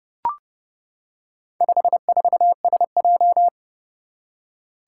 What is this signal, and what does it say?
Morse code, heard as a keyed tone, about 28 wpm, 700 Hz. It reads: 54SJ
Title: Four Challenging Letter-Number Combinations 30wpm